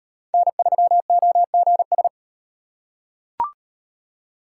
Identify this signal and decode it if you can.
Morse code, heard as a keyed tone, about 33 wpm, 700 Hz. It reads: N3OGS